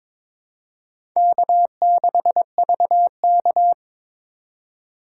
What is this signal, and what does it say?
Morse code, heard as a keyed tone, about 22 wpm, 700 Hz. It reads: K6VK
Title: USA Call Signs - 4 Characters - 2x Character Spacing - 22wpm